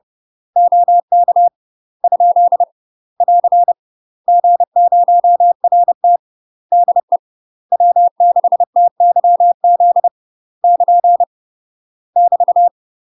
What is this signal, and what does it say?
Morse code, heard as a keyed tone, about 30 wpm, 700 Hz. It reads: OK ? <AR> G0RT DE W6TYZ <KN> <BT>
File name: UK MASTER QSOs TRAINING / 30WPM / UK MASTER QSOs 0004